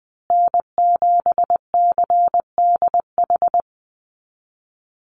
Morse code, heard as a keyed tone, about 20 wpm, 700 Hz. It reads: N7CDH
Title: USA Call Signs Repeated 3x 20wpm